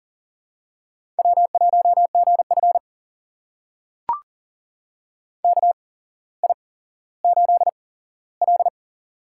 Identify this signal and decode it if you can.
Morse code, heard as a keyed tone, about 40 wpm, spaced as for 17 wpm, 700 Hz. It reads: W1GF KI8L
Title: USA Call Signs - 4 Characters - 8x Character Spacing - 40wpm